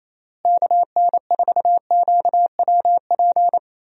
Morse code, heard as a keyed tone, about 28 wpm, 700 Hz. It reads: KN4QWP
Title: USA Call Signs - 6 Characters - Repeated 3x - 28wpm